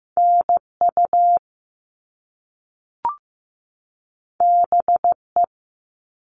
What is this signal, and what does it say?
Morse code, heard as a keyed tone, about 15 wpm, 700 Hz. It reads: NU BE
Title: All Cut Numbers - 2-Digit 15wpm